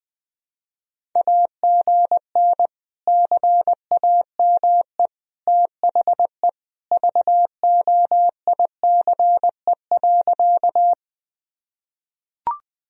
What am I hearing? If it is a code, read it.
Morse code, heard as a keyed tone, about 20 wpm, 700 Hz. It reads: AGN CAME THE VOICE.
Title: Sentences from Top 500 Words and Abbreviations 20wpm